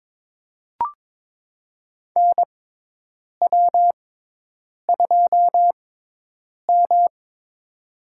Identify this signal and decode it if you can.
Morse code, heard as a keyed tone, about 22 wpm, spaced as for 8 wpm, 700 Hz. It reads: NW2M
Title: USA Call Signs - 6x Character Spacing - 22wpm